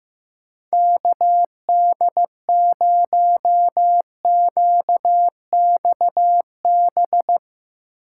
Morse code, heard as a keyed tone, about 17 wpm, 700 Hz. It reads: KD0QXB
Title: Speed Racing - USA 2x3 Call Signs 15wpm